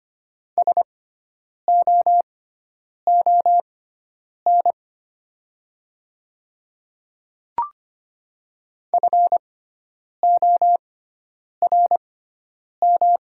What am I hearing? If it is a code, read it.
Morse code, heard as a keyed tone, about 25 wpm, spaced as for 9 wpm, 700 Hz. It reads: SOON FORM
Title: 6x Character Spacing - Top 500 Words - 25wpm